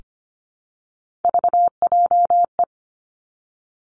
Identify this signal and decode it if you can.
Morse code, heard as a keyed tone, about 25 wpm, 700 Hz. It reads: VJE
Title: Three Challenging Letter-Number Combinations 25wpm